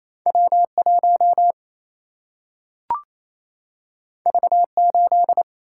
Morse code, heard as a keyed tone, about 28 wpm, 700 Hz. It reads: W1 V8
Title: Single-Letter Single-Number 28wpm